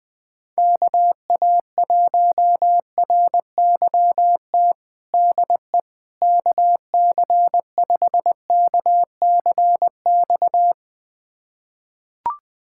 Morse code, heard as a keyed tone, about 20 wpm, 700 Hz. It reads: KA1RYT DE KC5KCX
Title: USA Callsign DE Callsign - 6 Characters - 20wpm